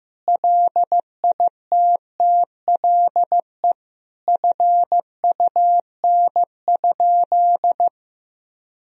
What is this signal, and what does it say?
Morse code, heard as a keyed tone, about 16 wpm, 700 Hz. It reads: LITTLE FUN?
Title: Sentences from Top 1000 Words - Review Part 1 - 15wpm